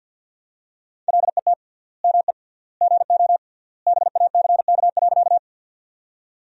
Morse code, heard as a keyed tone, about 52 wpm, 700 Hz. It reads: LET ME GO BACK.
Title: 3x Word-Spacing - Sentences from Top 300 Words - 50wpm